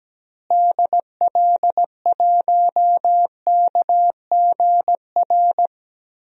Morse code, heard as a keyed tone, about 17 wpm, 700 Hz. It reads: DL1KGR
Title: DXCC-100 Call Signs - 6 Characters - Repeated 3x - 17wpm